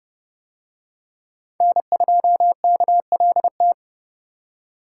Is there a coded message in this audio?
Yes: Morse code from 1.6 to 3.7 s, heard as a keyed tone, about 27 wpm, 700 Hz. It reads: N2KLT